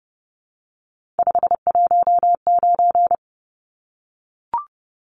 Morse code, heard as a keyed tone, about 30 wpm, 700 Hz. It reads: 519
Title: Numbers - 3-Digit 30wpm